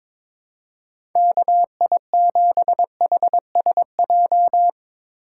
Morse code, heard as a keyed tone, about 21 wpm, 700 Hz. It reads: KI7HSJ
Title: USA Call Signs - 4x Character Spacing - 22wpm